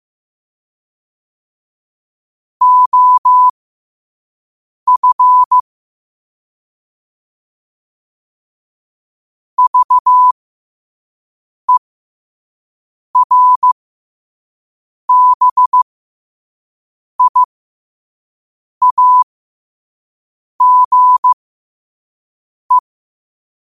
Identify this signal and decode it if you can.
Morse code, heard as a keyed tone, about 15 wpm, spaced as for 5 wpm, 1000 Hz. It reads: OF VERBIAGE
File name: qotdfiles / 5wpm